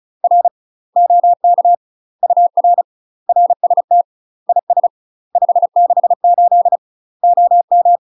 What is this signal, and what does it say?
Morse code, heard as a keyed tone, about 33 wpm, 700 Hz. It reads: R OK UR RST IS 568 OM